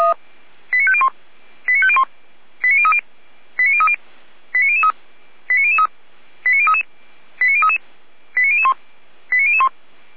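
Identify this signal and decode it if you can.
Five-tone selective call (ZVEI), heard as quick runs of five tones, tones of about 70 ms. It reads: 89501 89501 89030 89030 89003 89003 89033 89033 89001 89001
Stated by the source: Five-tone ZVEI selective call received by SDR, from the Feuerwehr Orlamünde server.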